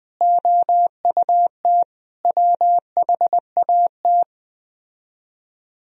Morse code, heard as a keyed tone, about 20 wpm, 700 Hz. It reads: OUT WHAT
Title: Speed-Racing - Sets of 5 Words 17wpm